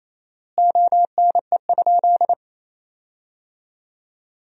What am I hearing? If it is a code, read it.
Morse code, heard as a keyed tone, about 27 wpm, 700 Hz. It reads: ONE?
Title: 4x Word-Spacing - Sentences from Top 100 Words - 28wpm